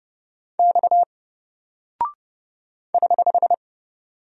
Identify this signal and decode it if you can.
Morse code, heard as a keyed tone, about 29 wpm, 700 Hz. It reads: X <HH>
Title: Single Character (Letter:Number:Punctuation:Prosign) 30wpm